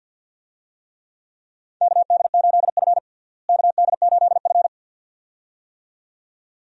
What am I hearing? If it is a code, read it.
Morse code, heard as a keyed tone, about 51 wpm, 700 Hz. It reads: KD8F KD8F
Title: USA Call Signs - 4 Characters - Repeated 2x - 50wpm